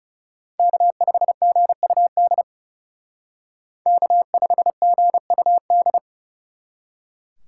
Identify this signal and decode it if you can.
Morse code, heard as a keyed tone, about 38 wpm, 700 Hz. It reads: K5GUD K5GUD